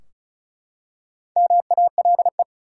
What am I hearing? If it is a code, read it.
Morse code, heard as a keyed tone, about 36 wpm, 700 Hz. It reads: MALE
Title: Four Letter Words - 35wpm